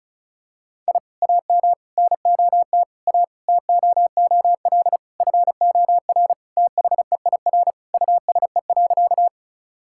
Morse code, heard as a keyed tone, about 34 wpm, 700 Hz. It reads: I AM NOT A TOOL FOR THEIR USE.
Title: Sentences from Top 1000 Words - Review Part 6 - 35wpm